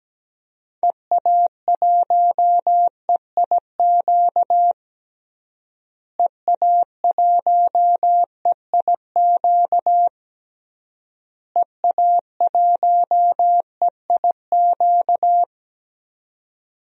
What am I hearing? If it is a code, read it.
Morse code, heard as a keyed tone, about 17 wpm, 700 Hz. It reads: EA1EIQ EA1EIQ EA1EIQ
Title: DXCC-100 Call Signs - 6 Characters - Repeated 3x - 17wpm